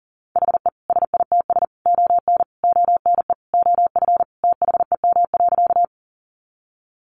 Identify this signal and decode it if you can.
Morse code, heard as a keyed tone, about 40 wpm, 700 Hz. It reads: HE SITS ON ONE OF THEM.
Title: Additional Easy Phrases 40wpm